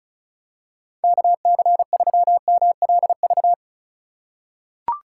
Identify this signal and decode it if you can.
Morse code, heard as a keyed tone, about 34 wpm, 700 Hz. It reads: KC3MLV
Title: USA Call Signs - 6 Characters - 2x Character Spacing - 35wpm